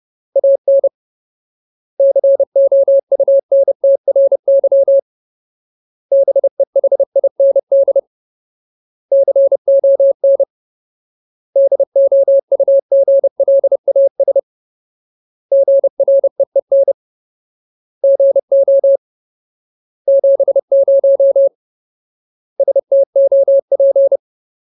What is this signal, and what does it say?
Morse code, heard as a keyed tone, about 30 wpm, 550 Hz. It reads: AN COUNTRY BEHIND CON DOUGLAS GREEN GO 70 STOP
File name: Ultimate Word List / 30WPM Ultimate